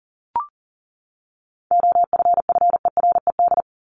Morse code, heard as a keyed tone, about 40 wpm, 700 Hz. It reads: OFFERED
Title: ED Suffix - Top 100 Words with Base Words 40wpm